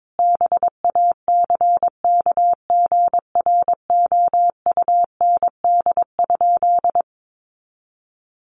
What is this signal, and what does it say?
Morse code, heard as a keyed tone, about 22 wpm, 700 Hz. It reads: BACKGROUND?
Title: Sentences from Top 1300 Words 22wpm